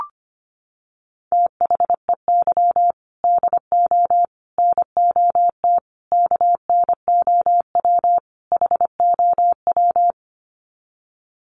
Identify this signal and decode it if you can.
Morse code, heard as a keyed tone, about 25 wpm, 700 Hz. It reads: THEY DO NOT KNOW HOW
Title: Sets of 5 Words Encore2 25wpm